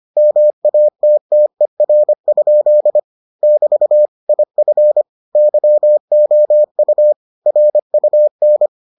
Morse code, heard as a keyed tone, about 25 wpm, 600 Hz. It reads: MATTER? <BT> IF YOU RUN